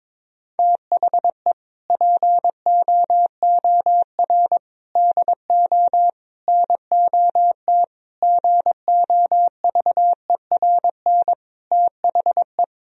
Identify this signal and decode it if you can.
Morse code, heard as a keyed tone, about 22 wpm, 700 Hz. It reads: THE POOR DO NOT GOVERN THE
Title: Sentences from Top 1000 Words - Review Part 2 - 22wpm